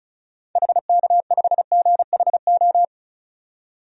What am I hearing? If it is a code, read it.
Morse code, heard as a keyed tone, about 36 wpm, 700 Hz. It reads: HK5GHO
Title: DXCC-100 Call Signs - 6 Characters - 35wpm